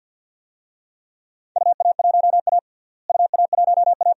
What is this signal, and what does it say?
Morse code, heard as a keyed tone, about 49 wpm, 700 Hz. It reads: UA1A UA1A
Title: DXCC-100 Call Signs - 4 Characters - Repeated 2x - 50wpm